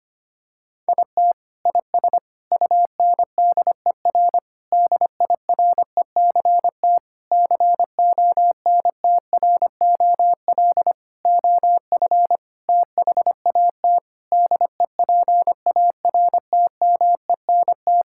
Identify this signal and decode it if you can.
Morse code, heard as a keyed tone, about 25 wpm, 700 Hz. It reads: IT IS UNDER DIRECT CONTROL OF THAT DEPARTMENT
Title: Sentences from Top 1000 Words - Review Part 5 - 25wpm